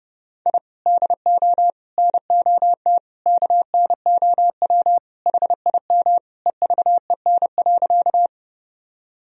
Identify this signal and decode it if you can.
Morse code, heard as a keyed tone, about 32 wpm, 700 Hz. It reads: I DO NOT KNOW HIM EVEN.
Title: Sentences from Top 1000 Words - Review Part 1 - 30wpm